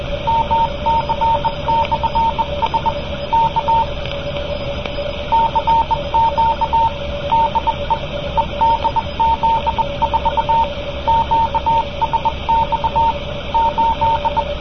Morse code, heard as a keyed tone, about 20 wpm, 910 Hz, with radio noise. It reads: MC/S K CQ DE EDZ4 QSX 8